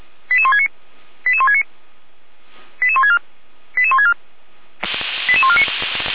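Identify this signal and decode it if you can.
Five-tone selective call (ZVEI), heard as quick runs of five tones, tones of about 70 ms. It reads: 88169 88169 88164 88164 88149
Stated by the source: ZVEI five-tone selective call received on an SDR.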